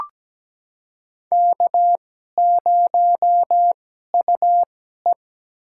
Morse code, heard as a keyed tone, about 17 wpm, spaced as for 12 wpm, 700 Hz. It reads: K0UE
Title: USA Call Signs - 4 Characters - 2x Character Spacing - 17wpm